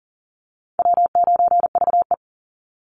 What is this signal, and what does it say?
Morse code, heard as a keyed tone, about 40 wpm, 700 Hz. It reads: W9VE
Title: Easy Call Signs Repeated 2x - 40wpm